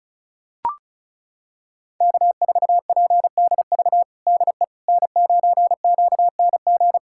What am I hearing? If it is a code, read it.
Morse code, heard as a keyed tone, about 36 wpm, 700 Hz. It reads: K4PDV DE N9QNG
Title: USA Callsign DE Callsign 35wpm